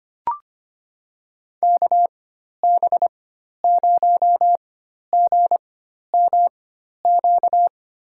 Morse code, heard as a keyed tone, about 25 wpm, spaced as for 12 wpm, 700 Hz. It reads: KB0GMQ
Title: USA Call Signs - 4x Character Spacing - 25wpm